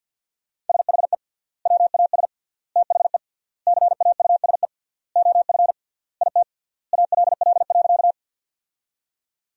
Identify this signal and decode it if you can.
Morse code, heard as a keyed tone, about 50 wpm, 700 Hz. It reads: SHE WAS THE CAUSE OF IT ALL.
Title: 3x Word-Spacing - Sentences from Top 200 Words - 50wpm